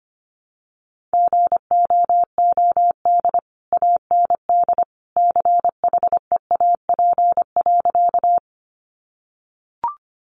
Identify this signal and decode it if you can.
Morse code, heard as a keyed tone, about 25 wpm, 700 Hz. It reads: GOOD AND CHEAP.
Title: Sentences from Top 1200 Words 25wpm